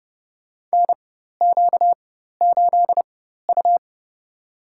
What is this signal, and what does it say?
Morse code, heard as a keyed tone, about 29 wpm, spaced as for 14 wpm, 700 Hz. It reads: NQ8U
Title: USA Call Signs - 4 Characters - 4x Character Spacing - 30wpm